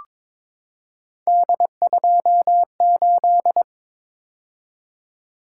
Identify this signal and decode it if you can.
Morse code, heard as a keyed tone, about 22 wpm, 700 Hz. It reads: D28
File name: practice / 22wpm / Three Letter-Number Combinations - 22wpm